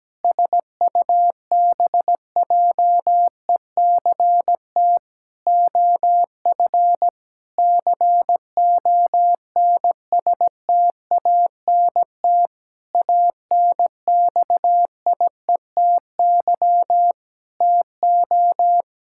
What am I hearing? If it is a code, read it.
Morse code, heard as a keyed tone, about 17 wpm, 700 Hz. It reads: SUBJECT OF CONSTANT ANXIETY TO
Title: Sentences from Top 1600 Words 17wpm